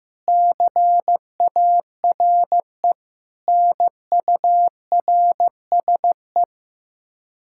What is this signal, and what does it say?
Morse code, heard as a keyed tone, about 15 wpm, 700 Hz. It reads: CARE NURSE
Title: Practice QSO - Exchange 3 - More Info - 15wpm